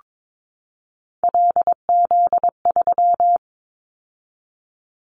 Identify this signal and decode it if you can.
Morse code, heard as a keyed tone, about 22 wpm, 700 Hz. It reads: LZ3
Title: Callsign Prefixes for Top 100 DXCC - 22wpm